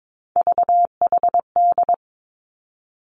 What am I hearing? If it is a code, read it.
Morse code, heard as a keyed tone, about 22 wpm, 700 Hz. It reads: VHD
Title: Three Challenging Letter-Number Combinations 22wpm